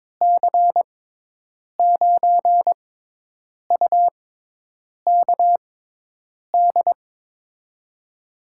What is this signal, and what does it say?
Morse code, heard as a keyed tone, about 21 wpm, spaced as for 8 wpm, 700 Hz. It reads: C9UKD
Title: USA Call Signs - 6 Characters - 6x Character Spacing - 22wpm